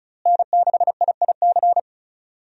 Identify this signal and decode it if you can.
Morse code, heard as a keyed tone, about 34 wpm, 700 Hz. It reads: N6IIC
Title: USA Call Signs - 5 Characters - Repeated 3x - 35wpm